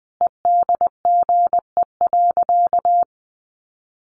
Morse code, heard as a keyed tone, about 20 wpm, 700 Hz. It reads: EDGE.
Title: Sentences from Top 1300 Words 20wpm